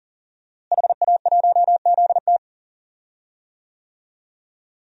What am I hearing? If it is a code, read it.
Morse code, heard as a keyed tone, about 41 wpm, 700 Hz. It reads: HA1ZT
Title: DXCC-100 Call Signs - 5 Characters - Repeated 2x - 40wpm